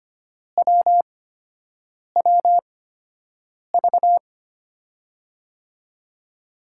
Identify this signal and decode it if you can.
Morse code, heard as a keyed tone, about 25 wpm, spaced as for 7 wpm, 700 Hz. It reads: WWV
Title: USA Call Signs - 8x Character Spacing - 25wpm